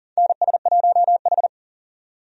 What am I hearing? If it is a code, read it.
Morse code, heard as a keyed tone, about 40 wpm, 700 Hz. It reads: NS1H
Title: USA Call Signs - 4 Characters - 40wpm